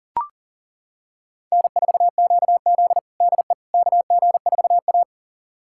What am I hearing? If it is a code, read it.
Morse code, heard as a keyed tone, about 40 wpm, 700 Hz. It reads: N4QZ DE KG4A
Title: USA Callsign DE Callsign - 4 Characters - 40wpm